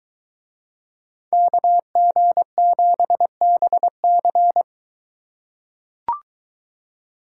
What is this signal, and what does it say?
Morse code, heard as a keyed tone, about 22 wpm, 700 Hz. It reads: KG7BC